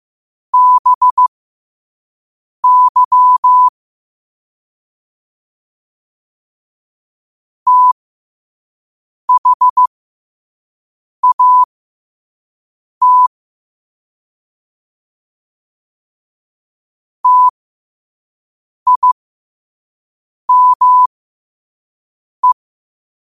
Morse code, heard as a keyed tone, about 15 wpm, spaced as for 5 wpm, 1000 Hz. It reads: BY THAT TIME
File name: qotdfiles / 5wpm